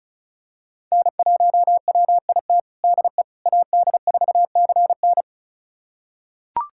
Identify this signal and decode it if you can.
Morse code, heard as a keyed tone, about 36 wpm, 690 Hz. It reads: N1WIT DE AD4CN
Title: USA Callsign DE Callsign - 5 Characters - 35wpm